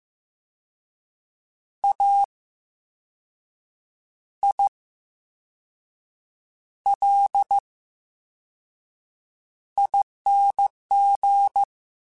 Morse code, heard as a keyed tone, about 15 wpm, spaced as for 5 wpm, 800 Hz. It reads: AILING
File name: Special INGs / Special INGs-mp3s / Slow ing @13